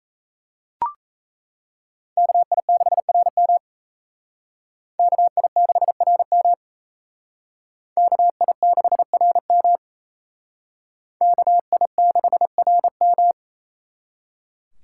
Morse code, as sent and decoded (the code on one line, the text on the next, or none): -.- .. -.... .-. -- / -.- .. -.... .-. -- / -.- .. -.... .-. -- / -.- .. -.... .-. --
KI6RM KI6RM KI6RM KI6RM